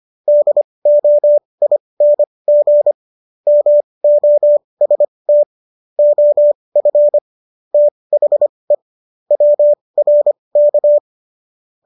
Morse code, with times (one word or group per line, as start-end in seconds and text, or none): 0.3-2.9 DOING
3.5-5.4 MOST
6.0-7.2 OF
7.7-8.8 THE
9.3-11.0 WRK